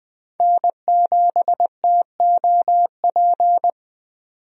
Morse code, heard as a keyed tone, about 20 wpm, 700 Hz. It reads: N7TOP
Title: USA Call Signs - 5 Characters - 20wpm